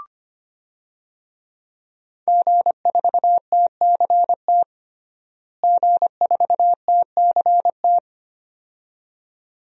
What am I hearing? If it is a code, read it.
Morse code, heard as a keyed tone, about 25 wpm, 700 Hz. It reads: G4TCT G4TCT
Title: DXCC-100 Call Signs - 5 Characters - Repeated 2x - 25wpm